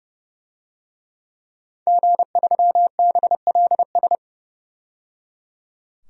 Morse code, heard as a keyed tone, about 30 wpm, 700 Hz. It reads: G3BLS
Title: DXCC-100 Call Signs - 5 Characters - 30wpm